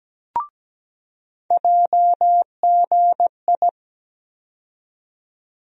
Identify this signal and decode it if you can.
Morse code, heard as a keyed tone, about 17 wpm, 700 Hz. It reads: JGI
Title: USA Call Sign Postfixes - 3 Letters 17wpm